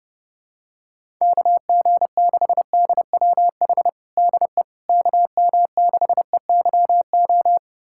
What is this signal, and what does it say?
Morse code, heard as a keyed tone, about 30 wpm, 700 Hz. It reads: KG6DWH DE KM6EYO
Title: USA Callsign DE Callsign - 6 Characters - 30wpm